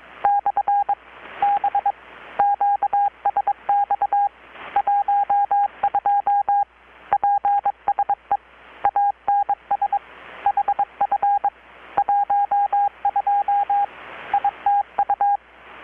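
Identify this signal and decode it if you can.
Morse code, heard as a keyed tone, about 22 wpm, 810 Hz, with radio noise. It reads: /B QSX 12 PSE ANS HF 12 ITU
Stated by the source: Morse from KPH's Night of Nights XX maritime radiotelegraphy event, received in Chiapas, Mexico about 3,800 km away on a Kenwood R-600 with a grounded 10 m long-wire antenna.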